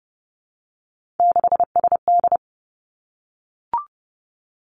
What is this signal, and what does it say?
Morse code, heard as a keyed tone, about 30 wpm, 700 Hz. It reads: 6SD
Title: Three Challenging Letter-Number Combinations 30wpm